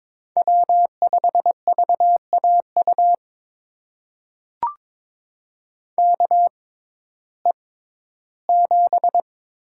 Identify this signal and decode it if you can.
Morse code, heard as a keyed tone, about 22 wpm, spaced as for 13 wpm, 700 Hz. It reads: W5VAU KE7
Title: USA Call Signs - 6x Character Spacing - 22wpm